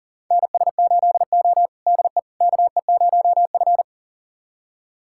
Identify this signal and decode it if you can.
Morse code, heard as a keyed tone, about 40 wpm, 700 Hz. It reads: NS8O DE KE0F
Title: USA Callsign DE Callsign - 4 Characters - 40wpm